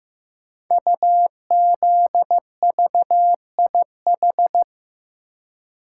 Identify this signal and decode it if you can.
Morse code, heard as a keyed tone, about 15 wpm, 700 Hz. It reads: UZVIH
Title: Code Groups - 5 Letters - 15wpm